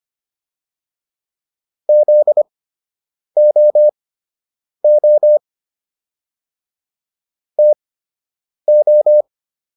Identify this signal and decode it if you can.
Morse code, heard as a keyed tone, about 25 wpm, spaced as for 8 wpm, 600 Hz. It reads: ZOO TO